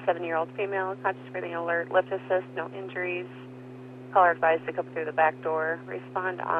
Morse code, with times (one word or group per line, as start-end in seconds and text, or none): none